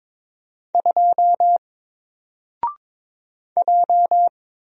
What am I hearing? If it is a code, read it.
Morse code, heard as a keyed tone, about 22 wpm, 700 Hz. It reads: 2 J
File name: Single-Letter-Number-22wpm